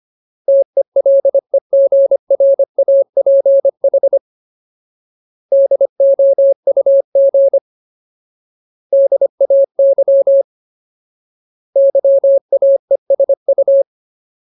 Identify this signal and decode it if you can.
Morse code, heard as a keyed tone, about 25 wpm, 550 Hz. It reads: TELEGRAPH DOUG DAY YAESU
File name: Ultimate Word List / 25WPM Ultimate